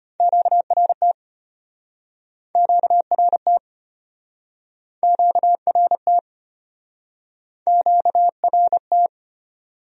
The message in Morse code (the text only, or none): QRT QRT QRT QRT